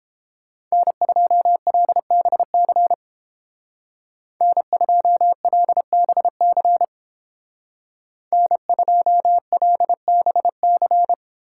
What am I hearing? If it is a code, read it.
Morse code, heard as a keyed tone, about 30 wpm, 700 Hz. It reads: N2LBC N2LBC N2LBC